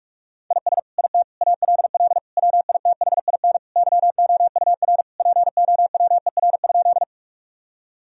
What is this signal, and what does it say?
Morse code, heard as a keyed tone, about 45 wpm, 700 Hz. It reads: IS IT ALL WITHIN YOUR POWER?
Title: Sentences from Top 1000 Words - Review Part 2 - 45wpm